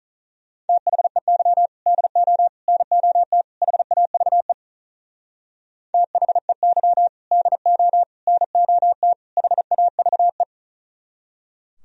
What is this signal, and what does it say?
Morse code, heard as a keyed tone, about 40 wpm, 700 Hz. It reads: THEY DO NOT HAVE THEY DO NOT HAVE
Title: Speed-Racing - Sets of 4 Words 35wpm